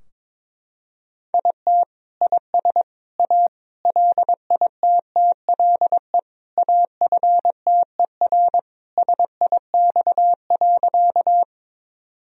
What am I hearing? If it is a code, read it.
Morse code, heard as a keyed tone, about 22 wpm, 700 Hz. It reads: IT IS A LITTLE AFTER SIX.
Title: Sentences from Top 1000 Words - Review Part 2 - 22wpm